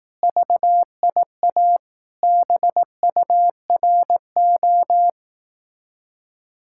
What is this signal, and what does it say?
Morse code, heard as a keyed tone, about 18 wpm, 700 Hz. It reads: VIA BURO
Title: Speed-Racing Common QSO Phrases Encore - 15wpm